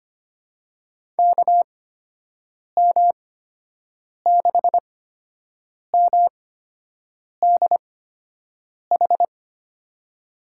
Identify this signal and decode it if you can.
Morse code, heard as a keyed tone, about 25 wpm, spaced as for 7 wpm, 700 Hz. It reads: KM6MDH
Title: USA Call Signs - 6 Characters - 8x Character Spacing - 25wpm